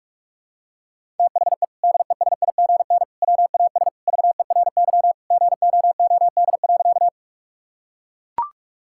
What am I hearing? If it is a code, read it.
Morse code, heard as a keyed tone, about 44 wpm, 700 Hz. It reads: THE DESIGN WAS VERY GOOD.
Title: Sentences from Top 1000 Words - Review Part 5 - 45wpm